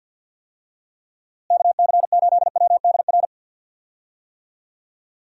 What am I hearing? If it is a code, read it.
Morse code, heard as a keyed tone, about 45 wpm, 700 Hz. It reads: KC8WDR
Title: USA Call Signs - 6 Characters - Repeated 2x - 50wpm